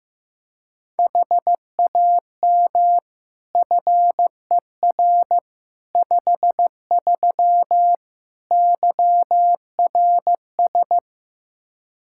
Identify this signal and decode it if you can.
Morse code, heard as a keyed tone, about 15 wpm, 700 Hz. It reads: HAM FER 53 YRS
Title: Practice QSO - Exchange 3 - More Info - 15wpm